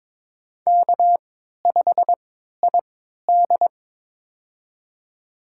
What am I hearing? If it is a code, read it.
Morse code, heard as a keyed tone, about 22 wpm, spaced as for 12 wpm, 700 Hz. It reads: K5ID
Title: USA Call Signs - 4 Characters - 3x Character Spacing - 22wpm